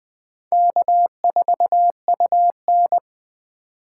Morse code, heard as a keyed tone, about 21 wpm, 700 Hz. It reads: K4UN